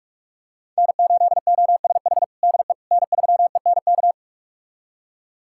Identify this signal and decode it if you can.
Morse code, heard as a keyed tone, about 46 wpm, 700 Hz. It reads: N8OSH DE N3ENK
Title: USA Callsign DE Callsign - 5 Characters - 45wpm